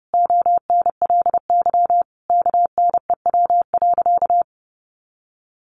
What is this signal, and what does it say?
Morse code, heard as a keyed tone, about 30 wpm, 700 Hz. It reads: ONLY KNEW.
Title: Sentences from Top 500 Words 30wpm